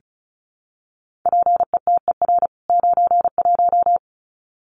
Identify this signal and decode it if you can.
Morse code, heard as a keyed tone, about 34 wpm, 700 Hz. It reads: PETER 91
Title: CWT Exchange 35wpm